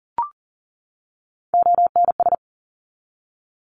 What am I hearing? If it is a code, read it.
Morse code, heard as a keyed tone, about 42 wpm, 700 Hz. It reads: ONS
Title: ARRL Sections 40wpm